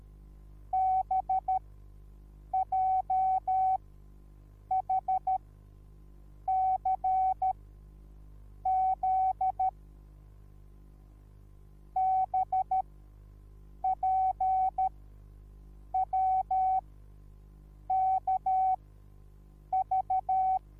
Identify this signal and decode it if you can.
Morse code, heard as a keyed tone, about 13 wpm, spaced as for 6 wpm, 760 Hz. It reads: BJHCZ BPWKV
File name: Onlinelehrgang / morsen / DJ4UF Morsen21